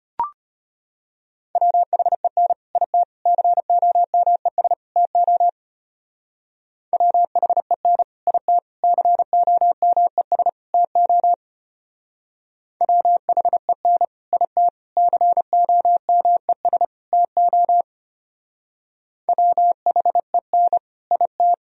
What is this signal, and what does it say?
Morse code, heard as a keyed tone, about 32 wpm, 700 Hz. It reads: WHEN IT COMES TO WHEN IT COMES TO WHEN IT COMES TO WHEN IT